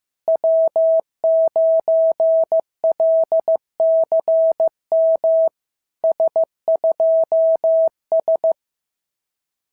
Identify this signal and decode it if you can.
Morse code, heard as a keyed tone, about 14 wpm, 650 Hz. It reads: W9LCM S2S
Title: SOTA QSO Practice 15wpm